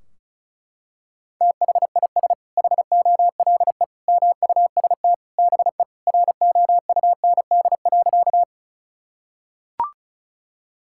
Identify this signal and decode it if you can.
Morse code, heard as a keyed tone, about 34 wpm, 700 Hz. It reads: THIS HOLE MUST BE ROUND.
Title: Sentences from Top 1600 Words 35wpm